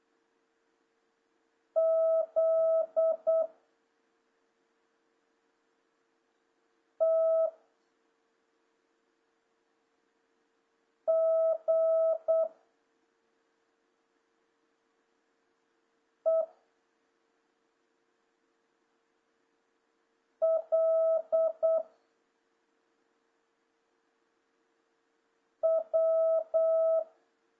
Morse code, heard as a keyed tone, about 8 wpm, 650 Hz. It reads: Z T G E L W
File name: jh8cbh / 2026cw